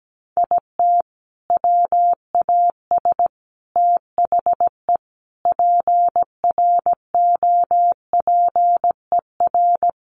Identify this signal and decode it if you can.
Morse code, heard as a keyed tone, about 17 wpm, 700 Hz. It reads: IT WAS THE PROPER
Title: Sentences from Top 1300 Words 17wpm